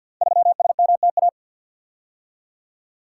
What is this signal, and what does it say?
Morse code, heard as a keyed tone, about 52 wpm, 700 Hz. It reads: 3SMTA
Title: Code Groups - 5 Letter:Numbers - 50wpm